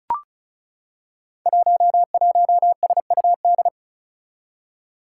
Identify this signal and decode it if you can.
Morse code, heard as a keyed tone, about 36 wpm, 700 Hz. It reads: 11SUD
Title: Five Challenging Letter-Number Combinations 35wpm